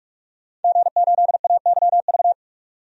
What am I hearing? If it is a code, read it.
Morse code, heard as a keyed tone, about 46 wpm, 700 Hz. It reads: G8AYV
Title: DXCC-100 Call Signs - 5 Characters - 45wpm